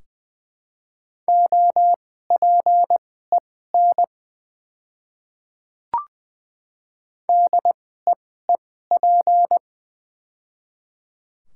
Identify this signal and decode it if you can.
Morse code, heard as a keyed tone, about 20 wpm, spaced as for 14 wpm, 700 Hz. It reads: OPEN DEEP
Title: 2x Character Spacing - Top 500 Words - 20wpm